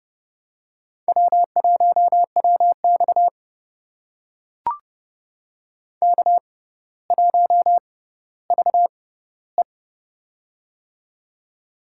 Morse code, heard as a keyed tone, about 30 wpm, spaced as for 15 wpm, 700 Hz. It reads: W1WX K1VE
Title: USA Call Signs - 6x Character Spacing - 30wpm